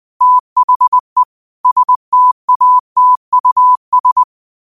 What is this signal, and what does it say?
Morse code, heard as a keyed tone, about 20 wpm, 1000 Hz. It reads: THE STATUS
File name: qotdfiles / 20wpm